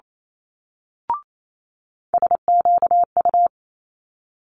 Morse code, heard as a keyed tone, about 28 wpm, 700 Hz. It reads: SQU
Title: Multi-letter Phonemes Quick Fire 28wpm